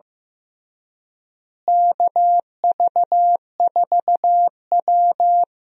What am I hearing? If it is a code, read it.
Morse code, heard as a keyed tone, about 15 wpm, 700 Hz. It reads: KV4W
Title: USA Call Signs - 4 Characters - Repeated 3x - 15wpm